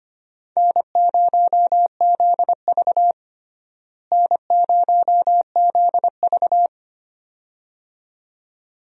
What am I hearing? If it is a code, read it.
Morse code, heard as a keyed tone, about 25 wpm, 700 Hz. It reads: N0ZV N0ZV
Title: USA Call Signs - 4 Characters - Repeated 2x - 25wpm